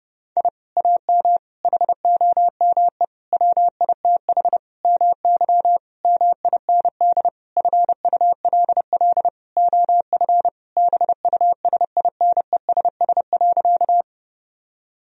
Morse code, heard as a keyed tone, about 32 wpm, 700 Hz. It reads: I AM HOME WITH MY MIND FULL OF BUSINESS.
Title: Sentences from Top 1000 Words - Review Part 3 - 30wpm